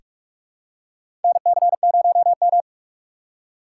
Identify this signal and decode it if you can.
Morse code, heard as a keyed tone, about 42 wpm, 700 Hz. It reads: NC0M
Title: USA Call Signs - 4 Characters - 3x Character Spacing - 45wpm